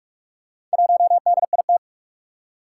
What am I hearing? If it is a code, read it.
Morse code, heard as a keyed tone, about 44 wpm, 700 Hz. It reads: 1DIT
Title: Four Challenging Letter-Number Combinations 45wpm